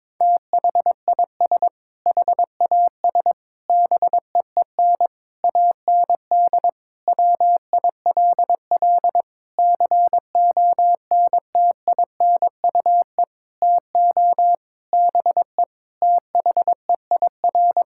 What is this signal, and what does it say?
Morse code, heard as a keyed tone, about 22 wpm, 700 Hz. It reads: THIS HAS BEEN AND WILL CONTINUE TO BE THEIR
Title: Sentences from Top 1600 Words 22wpm